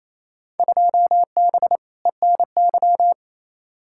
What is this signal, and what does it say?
Morse code, heard as a keyed tone, about 28 wpm, 700 Hz. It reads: 2B ENY
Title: Field Day Exchanges Encore - 28wpm